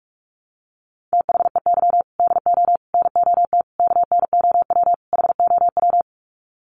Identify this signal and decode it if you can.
Morse code, heard as a keyed tone, about 44 wpm, 700 Hz. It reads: THEY DO NOT KNOW HOW
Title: Sets of 5 Words Encore2 45wpm